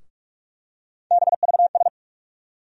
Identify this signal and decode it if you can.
Morse code, heard as a keyed tone, about 43 wpm, 700 Hz. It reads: BVS
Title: USA Call Sign Postfixes - 3 Letters 45wpm